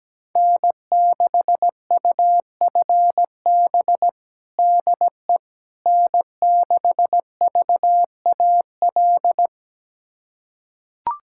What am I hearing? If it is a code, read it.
Morse code, heard as a keyed tone, about 17 wpm, 700 Hz. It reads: N6UFB DE N6VAL
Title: USA Callsign DE Callsign - 5 Characters - 17wpm